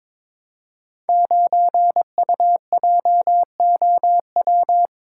Morse code, heard as a keyed tone, about 22 wpm, 700 Hz. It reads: 9UJOW
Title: Code Groups - 5 Letter:Numbers - 22wpm